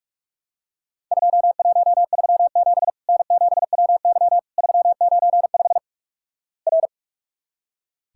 Morse code, heard as a keyed tone, about 46 wpm, 700 Hz. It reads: 2137 N7WY 395 R
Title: Worked All Europe DX Contest - QTC Exchange - 45wpm